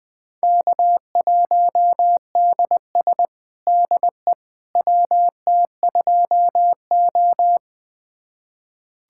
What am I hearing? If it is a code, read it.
Morse code, heard as a keyed tone, about 20 wpm, 700 Hz. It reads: K1DS DE WT2O
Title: USA Callsign DE Callsign - 4 Characters - 20wpm